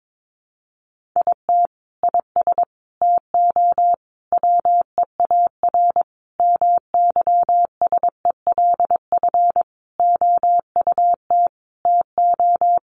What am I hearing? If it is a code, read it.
Morse code, heard as a keyed tone, about 22 wpm, 700 Hz. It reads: IT IS TO WEAR MYSELF OUT TO
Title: Sentences from Top 1500 Words 22wpm